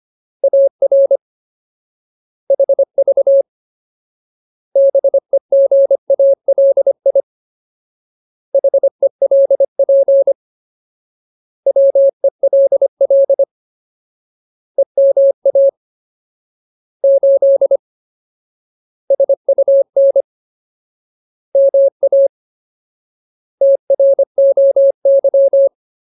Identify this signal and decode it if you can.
Morse code, heard as a keyed tone, about 25 wpm, 550 Hz. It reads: AR HV BEGALI HELP WELL EMA 8 SUN MA TROY
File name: Ultimate Word List / 25WPM Ultimate